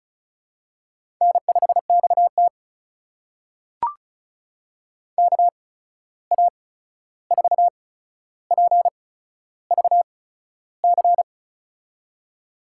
Morse code, heard as a keyed tone, about 36 wpm, spaced as for 13 wpm, 700 Hz. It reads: N5XT KA4PVC
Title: USA Call Signs - 8x Character Spacing - 35wpm